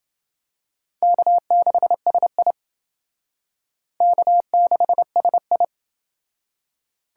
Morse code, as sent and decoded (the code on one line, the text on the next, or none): -.- -.... ... .. / -.- -.... ... ..
K6SI K6SI